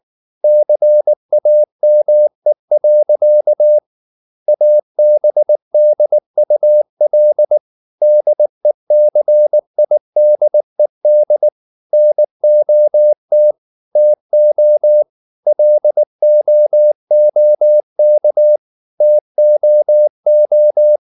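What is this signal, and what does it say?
Morse code, heard as a keyed tone, about 19 wpm, 600 Hz. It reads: CAME. ABDUL DECIDED NOT TO LOOK TOO
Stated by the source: Morse code at 19 wpm from CW med Gnister, reading Ken Follett's novel Never in English.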